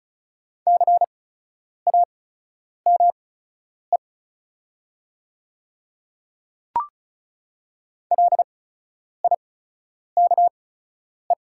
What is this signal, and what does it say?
Morse code, heard as a keyed tone, about 36 wpm, spaced as for 10 wpm, 700 Hz. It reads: CAME LIKE
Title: 8x Character Spacing - Top 300 Words - 35wpm